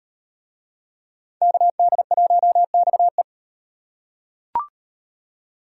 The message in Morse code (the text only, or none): KD1XE